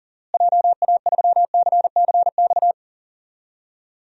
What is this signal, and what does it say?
Morse code, heard as a keyed tone, about 40 wpm, 700 Hz. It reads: JA3CCX
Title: DXCC-100 Call Signs - 6 Characters - Repeated 3x - 40wpm